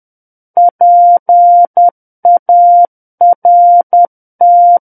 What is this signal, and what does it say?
Morse code, heard as a keyed tone, about 10 wpm, 700 Hz. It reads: PART
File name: Book One / 01-Chapter 1 - 10 WPM